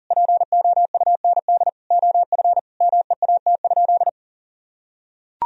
Morse code, heard as a keyed tone, about 40 wpm, 700 Hz. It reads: POUND OF MEAT?
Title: Sentences from Top 1000 Words - Review Part 3 - 40wpm